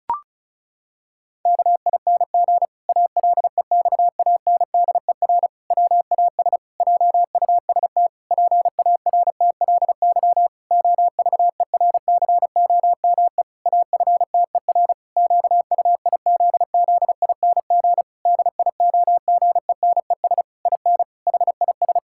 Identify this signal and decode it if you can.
Morse code, heard as a keyed tone, about 36 wpm, 700 Hz. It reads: KING ALEXANDER WAS JUST PARTLY OVERCOME AFTER QUIZZING DIOGENES IN HIS
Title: Pangram Challenge v2 - 35wpm